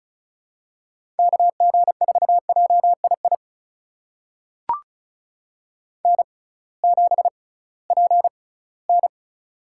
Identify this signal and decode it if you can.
Morse code, heard as a keyed tone, about 34 wpm, spaced as for 20 wpm, 700 Hz. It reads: KG4JII N7PN
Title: USA Call Signs - 6x Character Spacing - 35wpm